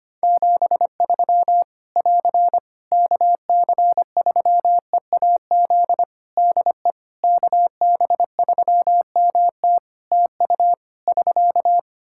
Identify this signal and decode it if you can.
Morse code, heard as a keyed tone, about 25 wpm, 700 Hz. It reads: 73 <AR> KC3EAZ DE KB3MT TU <SK>
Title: Practice QSO - Exchange 4 - Ending - 25wpm